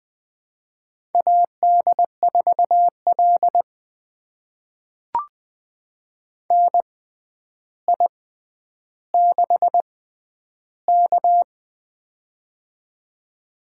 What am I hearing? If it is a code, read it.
Morse code, heard as a keyed tone, about 20 wpm, spaced as for 10 wpm, 700 Hz. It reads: AD4L NI6K
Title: USA Call Signs - 4 Characters - 6x Character Spacing - 20wpm